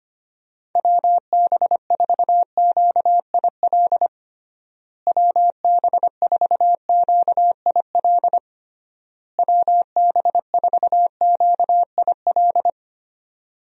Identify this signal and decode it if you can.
Morse code, heard as a keyed tone, about 25 wpm, 700 Hz. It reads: WB4QIL WB4QIL WB4QIL
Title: USA Call Signs - 6 Characters - Repeated 3x - 25wpm